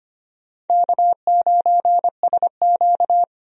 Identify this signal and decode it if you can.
Morse code, heard as a keyed tone, about 25 wpm, 700 Hz. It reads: K9SQ
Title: USA Call Signs - 4 Characters - 25wpm